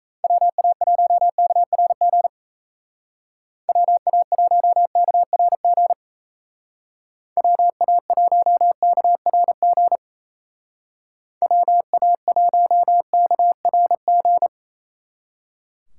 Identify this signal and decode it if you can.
Morse code, heard as a keyed tone, about 35 wpm, 700 Hz. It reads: WA1KRG WA1KRG WA1KRG WA1KRG
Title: Speed Racing - USA 2x3 Call Signs 28wpm